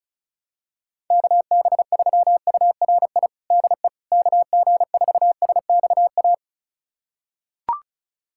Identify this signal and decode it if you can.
Morse code, heard as a keyed tone, about 36 wpm, 700 Hz. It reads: KB3URI DE KG4SXA
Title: USA Callsign DE Callsign 35wpm